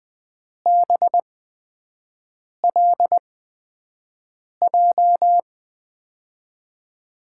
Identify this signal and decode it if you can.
Morse code, heard as a keyed tone, about 21 wpm, spaced as for 6 wpm, 700 Hz. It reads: BLJ
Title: USA Call Signs - 5 Characters - 8x Character Spacing - 20wpm